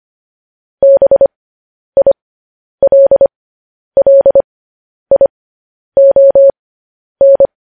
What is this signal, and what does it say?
Morse code, heard as a keyed tone, about 25 wpm, spaced as for 10 wpm, 550 Hz. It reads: BILLION